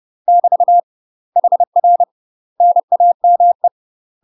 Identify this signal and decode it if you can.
Morse code, heard as a keyed tone, about 30 wpm, 700 Hz. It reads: <BT> HR NAME
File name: UK MASTER QSOs TRAINING / 30WPM / UK MASTER QSOs 0007